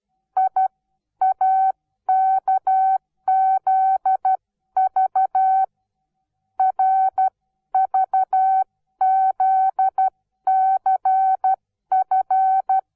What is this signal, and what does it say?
Morse code, heard as a keyed tone, about 12 wpm, spaced as for 11 wpm, 760 Hz. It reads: IAKZV RVZCF
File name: Onlinelehrgang / morsen / DJ4UF Morsen17